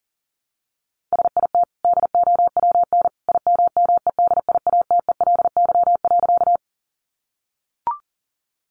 Morse code, heard as a keyed tone, about 40 wpm, 700 Hz. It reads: SIT DOWN IMMEDIATELY.
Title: Sentences from Top 1100 Words 40wpm